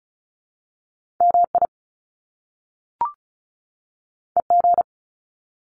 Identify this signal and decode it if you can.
Morse code, heard as a keyed tone, about 36 wpm, 700 Hz. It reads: MI EG
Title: Top Two Letter Combinations in Words 35wpm